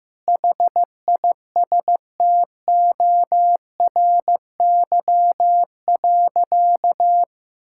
Morse code, heard as a keyed tone, about 15 wpm, 700 Hz. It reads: HISTORY.
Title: Sentences from Top 1000 Words - Review Part 5 - 15wpm